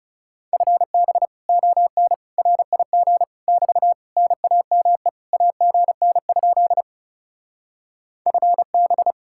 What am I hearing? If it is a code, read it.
Morse code, heard as a keyed tone, about 40 wpm, 700 Hz. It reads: FB ON RIG <BT> NAME AGN? FB